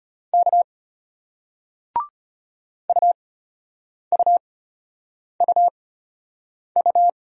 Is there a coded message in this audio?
Yes: Morse code from 0.3 to 7.1 s, heard as a keyed tone, about 33 wpm, spaced as for 8 wpm, 700 Hz. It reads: K UUUU